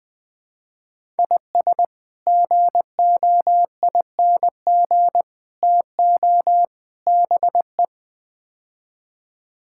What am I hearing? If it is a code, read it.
Morse code, heard as a keyed tone, about 20 wpm, 700 Hz. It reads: IS GOING TO BE
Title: Build A Phrase - 5-Word Phrases - 20wpm